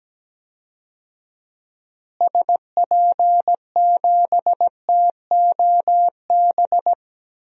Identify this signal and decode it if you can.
Morse code, heard as a keyed tone, about 17 wpm, 700 Hz. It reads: SP7TOB
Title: DXCC-100 Call Signs - 6 Characters - Repeated 3x - 17wpm